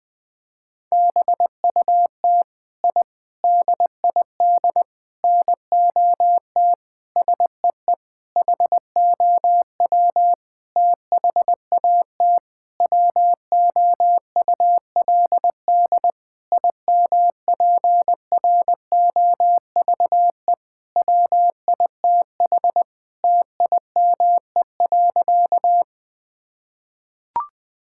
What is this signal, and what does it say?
Morse code, heard as a keyed tone, about 20 wpm, 700 Hz. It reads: BUT I DID NOT SEE HOW THAT WOULD IMPROVE WITH TIME.
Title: Sentences from Top 1000 Words - Review Part 5 - 20wpm